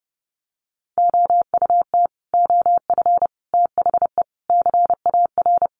Morse code, heard as a keyed tone, about 30 wpm, 700 Hz. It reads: OUT OF THE CAR
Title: Sets of 5 Words Encore 30wpm